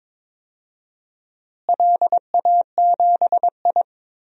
Morse code, heard as a keyed tone, about 21 wpm, 700 Hz. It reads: LA7I
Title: DXCC-100 Call Signs - 4 Characters - Repeated 3x - 22wpm